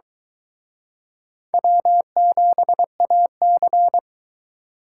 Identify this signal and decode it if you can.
Morse code, heard as a keyed tone, about 23 wpm, 700 Hz. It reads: W7AC